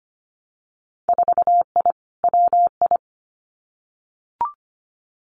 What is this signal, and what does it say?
Morse code, heard as a keyed tone, about 25 wpm, 700 Hz. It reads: 4I WI
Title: Winter Field Day Exchanges 25wpm